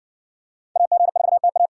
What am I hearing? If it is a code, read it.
Morse code, heard as a keyed tone, about 57 wpm, 700 Hz. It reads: AG4TA